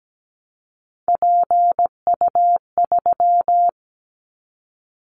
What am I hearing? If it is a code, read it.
Morse code, heard as a keyed tone, about 17 wpm, 700 Hz. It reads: PU3
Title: Callsign Prefixes for Top 100 DXCC - 17wpm